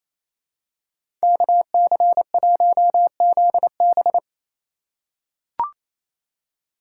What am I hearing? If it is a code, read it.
Morse code, heard as a keyed tone, about 28 wpm, 700 Hz. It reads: KC1ZB
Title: USA Call Signs - 5 Characters - 6x Character Spacing - 28wpm